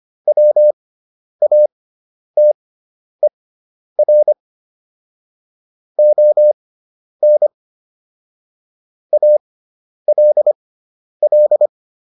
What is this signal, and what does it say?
Morse code, heard as a keyed tone, about 25 wpm, spaced as for 10 wpm, 600 Hz. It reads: WATER ON ALL